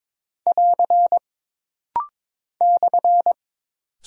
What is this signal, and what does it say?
Morse code, heard as a keyed tone, about 22 wpm, 700 Hz. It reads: <AR> /
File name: practice / 22wpm / Single Punctuation-Prosign - Rapid-Fire - 22wpm